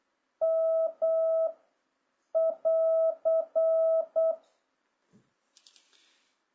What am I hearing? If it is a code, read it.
Morse code, heard as a keyed tone, about 8 wpm, 650 Hz. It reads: M <AR>
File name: jh8cbh / 2026cw